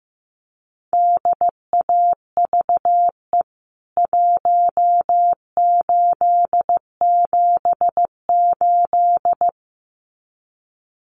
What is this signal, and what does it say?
Morse code, heard as a keyed tone, about 15 wpm, 700 Hz. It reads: DAVE 1878
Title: CWT Exchange 15wpm